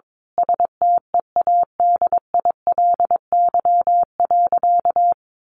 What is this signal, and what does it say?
Morse code, heard as a keyed tone, about 22 wpm, 700 Hz. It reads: STEADILY.
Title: A Sound Of Thunder 22wpm